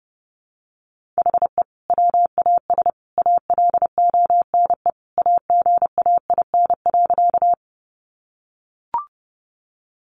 Morse code, heard as a keyed tone, about 30 wpm, 700 Hz. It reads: HE WAS ALONE AGAIN.
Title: Sentences from Top 1000 Words 30wpm